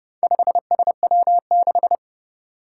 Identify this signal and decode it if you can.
Morse code, heard as a keyed tone, about 28 wpm, 700 Hz. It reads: 5SW6
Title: Four Challenging Letter-Number Combinations 30wpm